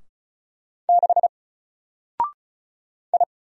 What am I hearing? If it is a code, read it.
Morse code, heard as a keyed tone, about 36 wpm, 700 Hz. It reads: 6 I
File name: Single-Letter-Number-35wpm